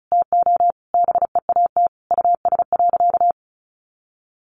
Morse code, heard as a keyed tone, about 34 wpm, 700 Hz. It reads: TO BEAT US.
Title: Sentences from Top 1300 Words 35wpm